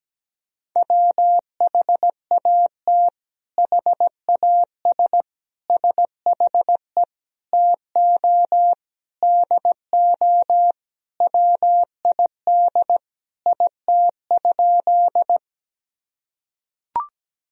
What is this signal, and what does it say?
Morse code, heard as a keyed tone, about 17 wpm, 700 Hz. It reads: WHAT HAS SHE TO DO WID IT?
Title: Sentences from Top 100 Words and Abbreviations 17wpm